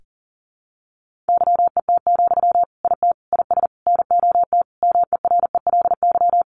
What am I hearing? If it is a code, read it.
Morse code, heard as a keyed tone, about 40 wpm, 700 Hz. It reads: YET, IT IS NOT MERELY
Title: Sentences from Top 1400 Words 40wpm